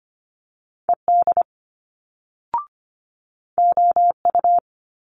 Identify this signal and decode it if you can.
Morse code, heard as a keyed tone, about 25 wpm, 700 Hz. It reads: ED OU
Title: Multi-letter Phonemes Quick Fire 25wpm